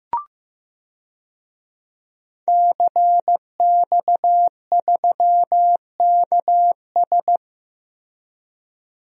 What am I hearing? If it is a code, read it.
Morse code, heard as a keyed tone, about 15 wpm, 700 Hz. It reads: CX3KS
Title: DXCC-100 Call Signs - 5 Characters - Repeated 2x - 15wpm